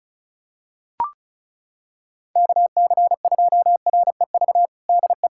Morse code, heard as a keyed tone, about 36 wpm, 700 Hz. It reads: KC2REV DE
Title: USA Callsign DE Callsign - 6 Characters - 35wpm